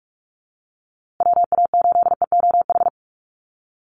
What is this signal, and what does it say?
Morse code, heard as a keyed tone, about 44 wpm, 700 Hz. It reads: WA8EOH
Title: USA Call Signs Repeated 2x 45wpm